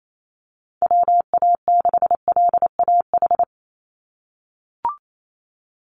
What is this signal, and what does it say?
Morse code, heard as a keyed tone, about 28 wpm, 700 Hz. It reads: WA6LAH
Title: USA Call Signs 28wpm